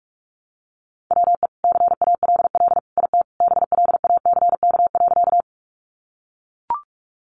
Morse code, heard as a keyed tone, about 44 wpm, 700 Hz. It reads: WE CALL IT BLACK.
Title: Sentences from Top 400 Words 45wpm